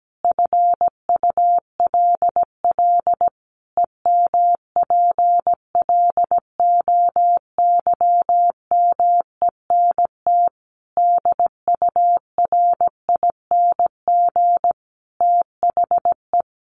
Morse code, heard as a keyed tone, about 17 wpm, 700 Hz. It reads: FULL EMPLOYMENT DURING THE
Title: Sentences from Top 1500 Words 17wpm